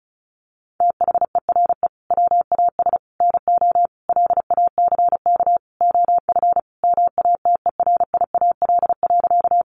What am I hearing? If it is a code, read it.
Morse code, heard as a keyed tone, about 34 wpm, 700 Hz. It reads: THERE WAS NO LACK OF MATERIAL.
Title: Sentences from Top 1100 Words 35wpm